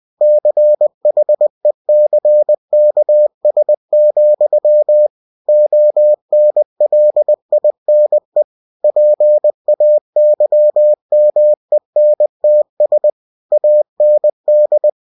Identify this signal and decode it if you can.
Morse code, heard as a keyed tone, about 20 wpm, 600 Hz. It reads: CHECKS, ONLINE PAYMENTS AND
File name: cw-books / winnie-the-pooh / winnie-20wpm-page-0122